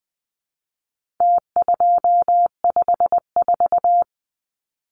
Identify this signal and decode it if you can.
Morse code, heard as a keyed tone, about 20 wpm, 700 Hz. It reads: T254
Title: Numbers - Contest Style - 4-Digit 20wpm